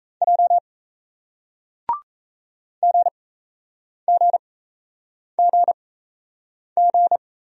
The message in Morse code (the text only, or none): J GGGG